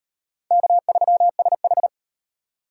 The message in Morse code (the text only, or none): K3SH